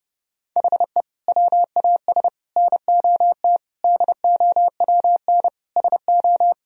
Morse code, heard as a keyed tone, about 30 wpm, 700 Hz. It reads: HE WAS NOT DOWN SO
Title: Sentences from Top 1000 Words - Review Part 1 - 30wpm